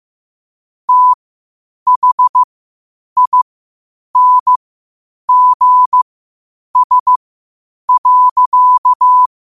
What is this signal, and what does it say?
Morse code, heard as a keyed tone, about 15 wpm, spaced as for 8 wpm, 1000 Hz. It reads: THINGS.
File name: qotdfiles / 8wpm